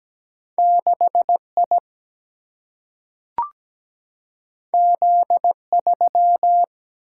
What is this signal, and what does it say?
Morse code, heard as a keyed tone, about 17 wpm, 700 Hz. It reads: 6 I Z 3
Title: Single-Letter Single-Number 17wpm